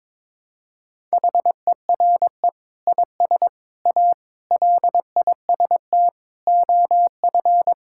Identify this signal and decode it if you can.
Morse code, heard as a keyed tone, about 22 wpm, 700 Hz. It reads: HERE IS A LIST OF
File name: practice / 22wpm / Sets of 8 Words - 22wpm